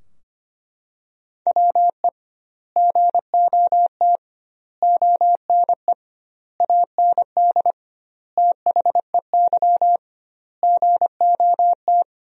Morse code, heard as a keyed tone, about 25 wpm, 700 Hz. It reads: WE GOT ONE AND THEY GOT
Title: 2x Word-Spacing - Sentences from Top 300 Words - 25wpm